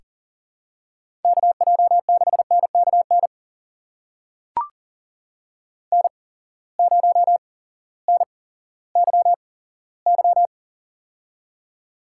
Morse code, heard as a keyed tone, about 40 wpm, spaced as for 18 wpm, 700 Hz. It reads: KJ6NKN N0NYY
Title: USA Call Signs - 8x Character Spacing - 40wpm